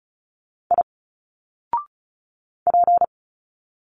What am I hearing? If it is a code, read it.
Morse code, heard as a keyed tone, about 36 wpm, 700 Hz. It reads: I P
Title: Single Letters 35wpm